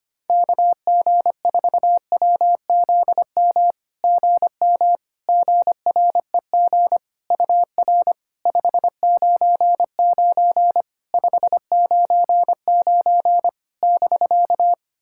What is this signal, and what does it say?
Morse code, heard as a keyed tone, about 25 wpm, 700 Hz. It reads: KG4WZM GM GREG UR 599 599 <BK>
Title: SOTA QSO Practice 25wpm